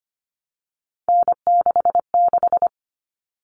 Morse code, heard as a keyed tone, about 25 wpm, 700 Hz. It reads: N66
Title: All Cut Numbers - 3-Digit 25wpm